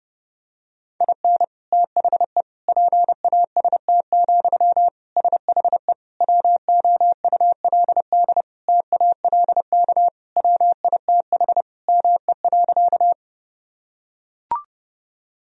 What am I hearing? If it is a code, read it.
Morse code, heard as a keyed tone, about 30 wpm, 700 Hz. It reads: IN THE PAST, SHE WOULD TALK WITH ME.
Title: Sentences from Top 1000 Words - Review Part 3 - 30wpm